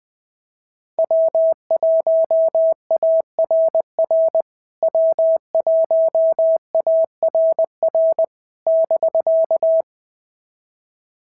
Morse code, heard as a keyed tone, about 21 wpm, 650 Hz. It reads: W1ARR W1ARR <BK>
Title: POTA QSO Practice - 20wpm